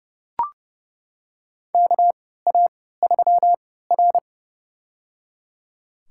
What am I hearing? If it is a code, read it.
Morse code, heard as a keyed tone, about 30 wpm, spaced as for 17 wpm, 700 Hz. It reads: KA3R
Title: USA Call Signs - 4 Characters - 3x Character Spacing - 30wpm